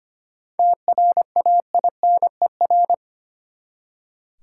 Morse code, heard as a keyed tone, about 25 wpm, 700 Hz. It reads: TRAINER
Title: Seven Letter Words - 25wpm